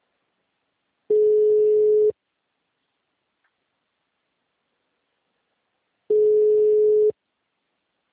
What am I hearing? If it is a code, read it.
Ringback tone (Europe, 425 Hz).